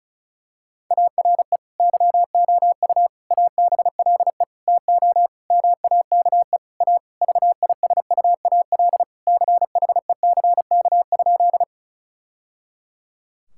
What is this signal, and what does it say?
Morse code, heard as a keyed tone, about 36 wpm, 700 Hz. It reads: ARE YOU ABLE TO MAKE A VISUAL CHECK?
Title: Sentences from Top 1000 Words - Review Part 6 - 35wpm